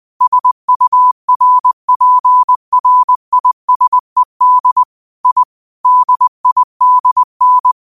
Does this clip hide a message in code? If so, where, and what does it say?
Yes: Morse code from 0.2 to 7.7 s, heard as a keyed tone, about 20 wpm, 1000 Hz. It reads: SURPRISED I DIDN